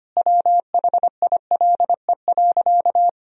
Morse code, heard as a keyed tone, about 25 wpm, 700 Hz. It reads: WHILE.
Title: Sentences from Top 500 Words and Abbreviations 25wpm